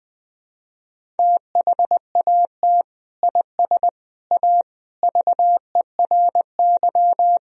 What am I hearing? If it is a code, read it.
Morse code, heard as a keyed tone, about 20 wpm, 700 Hz. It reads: THAT IS A VERY
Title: Sentences from Top 1000 Words - Review Part 1 - 20wpm